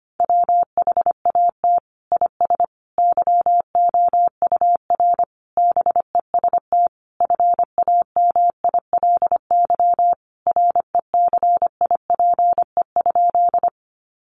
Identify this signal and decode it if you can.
Morse code, heard as a keyed tone, about 25 wpm, 700 Hz. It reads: WHAT IS YOUR BEST FAMILY RECIPE?
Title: Sentences from Top 1200 Words 25wpm